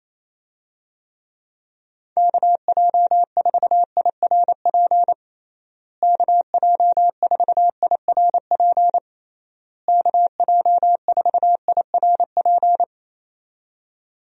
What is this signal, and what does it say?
Morse code, heard as a keyed tone, about 28 wpm, 700 Hz. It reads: KJ4IRP KJ4IRP KJ4IRP
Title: USA Call Signs - 6 Characters - Repeated 3x - 28wpm